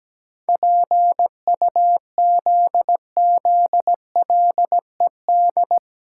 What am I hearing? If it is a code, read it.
Morse code, heard as a keyed tone, about 17 wpm, 700 Hz. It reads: PUZZLED
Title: Pangram Challenge v2 - 17wpm